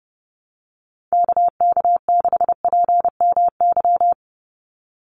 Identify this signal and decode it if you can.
Morse code, heard as a keyed tone, about 30 wpm, 700 Hz. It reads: KK6PMY
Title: USA Call Signs 30wpm